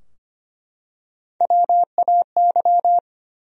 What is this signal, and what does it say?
Morse code, heard as a keyed tone, about 25 wpm, 700 Hz. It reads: WAY
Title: Top 100 Words Repeated 25wpm